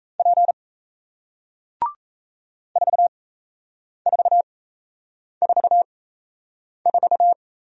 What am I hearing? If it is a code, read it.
Morse code, heard as a keyed tone, about 37 wpm, spaced as for 8 wpm, 700 Hz. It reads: P 4444